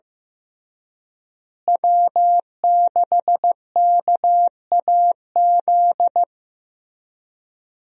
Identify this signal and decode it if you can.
Morse code, heard as a keyed tone, about 15 wpm, 700 Hz. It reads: W6KAZ
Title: USA Call Signs - 5 Characters - Repeated 3x - 15wpm